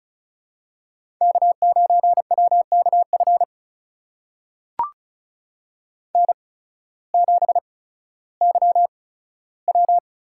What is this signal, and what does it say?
Morse code, heard as a keyed tone, about 34 wpm, spaced as for 16 wpm, 700 Hz. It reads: K9WKF N7YW
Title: USA Call Signs - 5 Characters - 8x Character Spacing - 35wpm